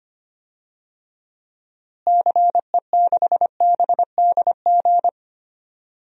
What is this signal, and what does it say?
Morse code, heard as a keyed tone, about 25 wpm, 700 Hz. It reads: CE6BDG
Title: DXCC-100 Call Signs - 6 Characters - Repeated 3x - 25wpm